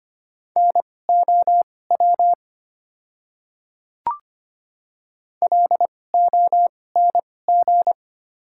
Morse code, heard as a keyed tone, about 25 wpm, spaced as for 18 wpm, 700 Hz. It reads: NOW LONG
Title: 2x Character Spacing - Top 200 Words - 25wpm